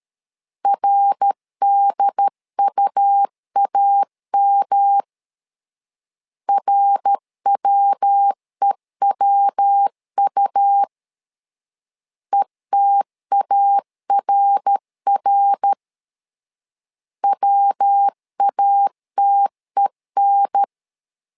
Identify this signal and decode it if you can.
Morse code, heard as a keyed tone, about 13 wpm, 800 Hz. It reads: RDUAM RWEWU ETARR WATEN
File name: LEZIONI / 3 MWRO / 3 MWR 12wpm 1711